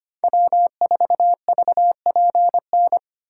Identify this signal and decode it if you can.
Morse code, heard as a keyed tone, about 25 wpm, 700 Hz. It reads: W4VPN
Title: USA Callsign DE Callsign - 5 Characters - 25wpm